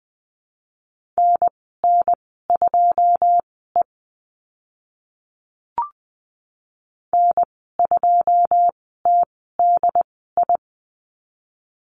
Morse code, heard as a keyed tone, about 20 wpm, spaced as for 15 wpm, 700 Hz. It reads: NN2E N2TDI
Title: 082 Lesson 14g - Number 2 - Call Signs 20wpm